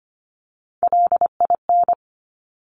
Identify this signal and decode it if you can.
Morse code, heard as a keyed tone, about 25 wpm, 700 Hz. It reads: LIN
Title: Top Three Letter Combinations in Words 25wpm